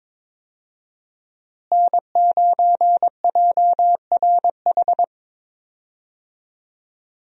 Morse code, heard as a keyed tone, about 21 wpm, 700 Hz. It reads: N9JRH
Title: USA Call Signs - 5 Characters - Repeated 2x - 22wpm